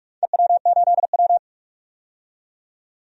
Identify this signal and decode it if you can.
Morse code, heard as a keyed tone, about 41 wpm, 700 Hz. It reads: EW8W
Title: DXCC-100 Call Signs - 4 Characters - 45wpm